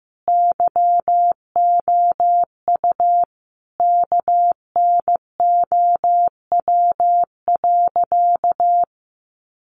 Morse code, heard as a keyed tone, about 15 wpm, 700 Hz. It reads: YOU KNOW.
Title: Sentences from Top 700 Words 15wpm